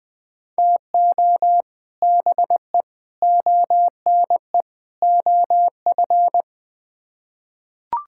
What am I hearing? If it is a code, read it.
Morse code, heard as a keyed tone, about 21 wpm, 700 Hz. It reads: TO BE ONE OF
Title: Build A Phrase - 4-Word Phrases - 20wpm